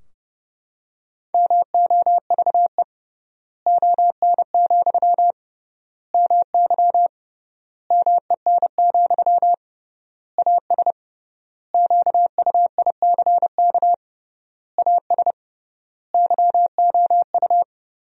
Morse code, heard as a keyed tone, about 30 wpm, 700 Hz. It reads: MOVE ON, MY MEN, AS QUICK AS YOU
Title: 3x Word-Spacing - Sentences from Top 500 Words - 30wpm